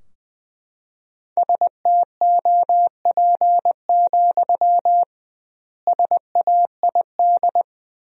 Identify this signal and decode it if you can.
Morse code, heard as a keyed tone, about 21 wpm, 700 Hz. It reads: STOP, SAID
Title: 2x Word-Spacing - Sentences from Top 300 Words - 20wpm